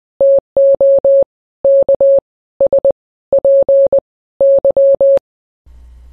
Morse code, heard as a keyed tone, about 20 wpm, 550 Hz. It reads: TO K S P Y
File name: radio / Imaginary Stations / KSPY 9395 kHz 29.1.23